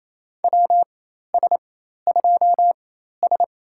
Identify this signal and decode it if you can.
Morse code, heard as a keyed tone, about 27 wpm, spaced as for 13 wpm, 700 Hz. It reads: WS2S
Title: USA Call Signs - 4x Character Spacing - 28wpm